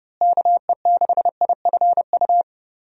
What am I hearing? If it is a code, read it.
Morse code, heard as a keyed tone, about 30 wpm, 700 Hz. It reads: KE6IFU
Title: USA Call Signs - 6 Characters - Repeated 2x - 30wpm